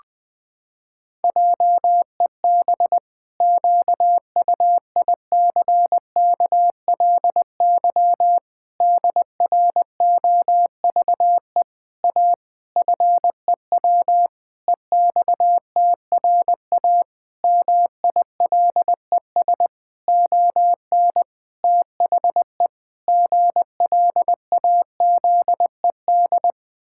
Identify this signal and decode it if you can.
Morse code, heard as a keyed tone, about 20 wpm, 700 Hz. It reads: JEB QUICKLY DROVE A FEW EXTRA MILES ON THE GLAZED
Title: Pangram Challenge v2 - 20wpm